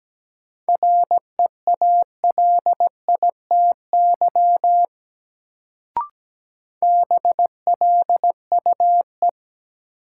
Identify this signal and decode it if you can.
Morse code, heard as a keyed tone, about 17 wpm, 700 Hz. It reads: REALITY BLUE
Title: Sentences from Top 1200 Words - New Word Drill Alt - 17wpm